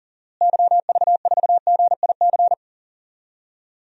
Morse code, heard as a keyed tone, about 40 wpm, 700 Hz. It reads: YV4GIC
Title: DXCC-100 Call Signs - 6 Characters - Repeated 3x - 40wpm